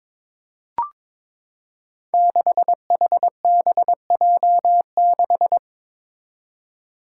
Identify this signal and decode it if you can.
Morse code, heard as a keyed tone, about 22 wpm, 700 Hz. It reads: 6HBJ6
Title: Five Challenging Letter-Number Combinations 22wpm